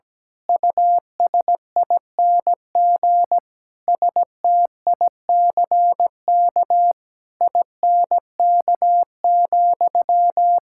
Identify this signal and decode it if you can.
Morse code, heard as a keyed tone, about 17 wpm, 700 Hz. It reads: USING STICK INK,
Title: Pangram Challenge v2 - 17wpm